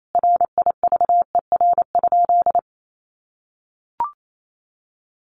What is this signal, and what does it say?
Morse code, heard as a keyed tone, about 28 wpm, 700 Hz. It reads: RIVER?
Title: Sentences from Top 1400 Words 28wpm